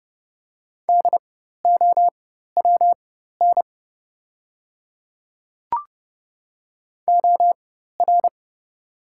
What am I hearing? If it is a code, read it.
Morse code, heard as a keyed tone, about 30 wpm, spaced as for 14 wpm, 700 Hz. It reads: DOWN OR
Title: 4x Character Spacing - Top 100 Words - 30wpm